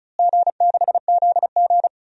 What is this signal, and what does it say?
Morse code, heard as a keyed tone, about 36 wpm, 700 Hz. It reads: G6ZG
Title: DXCC-100 Call Signs - 4 Characters - Repeated 3x - 35wpm